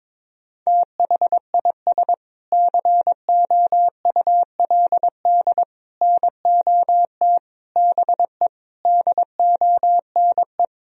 Morse code, heard as a keyed tone, about 22 wpm, 700 Hz. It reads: THIS COULD NOT BE DONE
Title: Sentences from Top 1000 Words - Review Part 3 - 22wpm